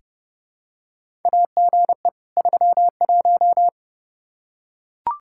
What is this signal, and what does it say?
Morse code, heard as a keyed tone, about 31 wpm, 700 Hz. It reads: AGE 31
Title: Practice QSO - Exchange 3 - More Info - 30wpm